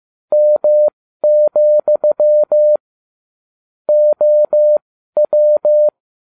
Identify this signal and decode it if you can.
Morse code, heard as a keyed tone, about 15 wpm, spaced as for 12 wpm, 600 Hz. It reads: M, OW